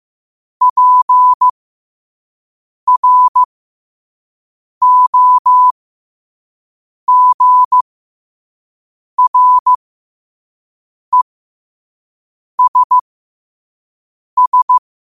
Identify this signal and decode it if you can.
Morse code, heard as a keyed tone, about 15 wpm, spaced as for 5 wpm, 1000 Hz. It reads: PROGRESS
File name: qotdfiles / 5wpm